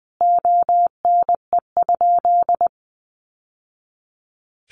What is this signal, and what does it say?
Morse code, heard as a keyed tone, about 20 wpm, 700 Hz. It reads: ONE?
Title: Sentences from Top 900 Words 20wpm